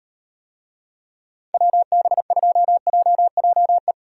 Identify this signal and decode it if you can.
Morse code, heard as a keyed tone, about 36 wpm, 700 Hz. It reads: WB2JJE